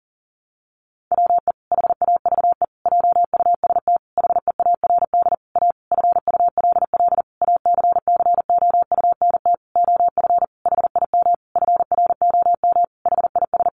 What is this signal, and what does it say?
Morse code, heard as a keyed tone, about 40 wpm, 700 Hz. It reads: WE HAVE JUST HEARD A FULL ACCOUNT OF HIM FROM HIS
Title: Sentences from Top 900 Words 40wpm